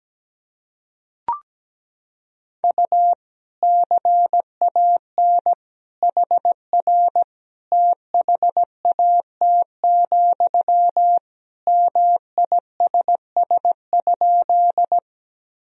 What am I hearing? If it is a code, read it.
Morse code, heard as a keyed tone, about 17 wpm, 700 Hz. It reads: U CAN HR THAT, MISS?
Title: Sentences from Top 500 Words and Abbreviations 17wpm